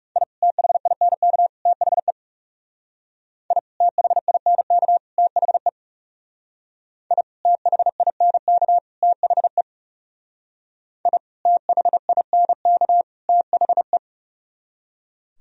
Morse code, heard as a keyed tone, about 40 wpm, 690 Hz. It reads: I THINK THE I THINK THE I THINK THE I THINK THE